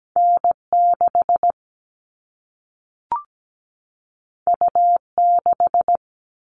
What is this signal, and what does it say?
Morse code, heard as a keyed tone, about 17 wpm, 700 Hz. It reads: N6 U6
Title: All Cut Numbers - 2-Digit 17wpm